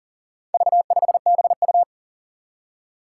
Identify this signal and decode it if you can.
Morse code, heard as a keyed tone, about 40 wpm, 700 Hz. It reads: V5BU
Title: Four Challenging Letter-Number Combinations 40wpm